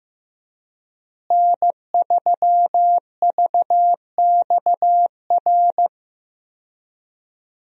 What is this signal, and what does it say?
Morse code, heard as a keyed tone, about 15 wpm, 700 Hz. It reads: N3VXR
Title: USA Call Signs - 5 Characters - Repeated 3x - 15wpm